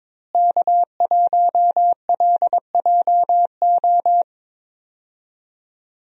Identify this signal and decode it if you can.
Morse code, heard as a keyed tone, about 21 wpm, 700 Hz. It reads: K1LJO
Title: USA Call Signs - 5 Characters - Repeated 2x - 22wpm